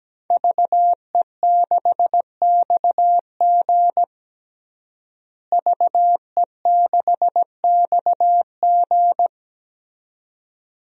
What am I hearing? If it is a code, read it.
Morse code, heard as a keyed tone, about 17 wpm, 700 Hz. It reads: VE6XG VE6XG
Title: DXCC-100 Call Signs - 5 Characters - Repeated 2x - 17wpm